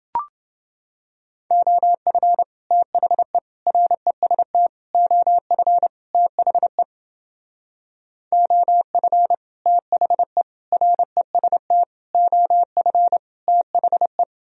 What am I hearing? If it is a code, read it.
Morse code, heard as a keyed tone, about 30 wpm, 700 Hz. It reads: OF THE REST OF THE OF THE REST OF THE